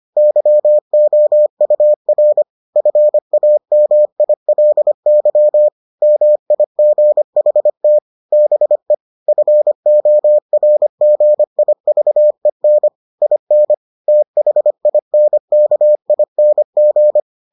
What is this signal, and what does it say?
Morse code, heard as a keyed tone, about 25 wpm, 600 Hz. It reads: YOUR FAMILY MIGHT BE FORGIVEN IN THINKING